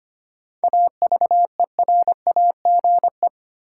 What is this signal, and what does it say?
Morse code, heard as a keyed tone, about 25 wpm, 700 Hz. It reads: AVERAGE
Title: Sentences from Top 1400 Words - New Word Drill Alt - 25wpm